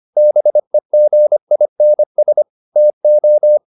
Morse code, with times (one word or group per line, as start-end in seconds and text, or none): 0.2-2.4 BEGINS
2.8-3.6 TO